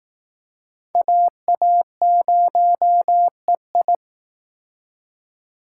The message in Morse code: AA0EI

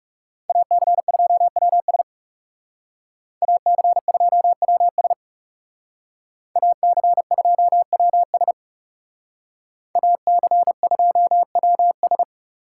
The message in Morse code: AC2WS AC2WS AC2WS AC2WS

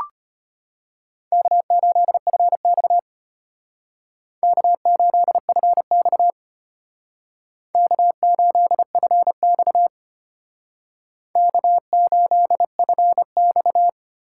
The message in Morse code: K8FX K8FX K8FX K8FX

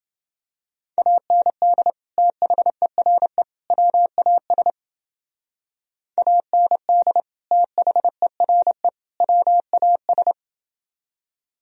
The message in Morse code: AND THERE WAS AND THERE WAS

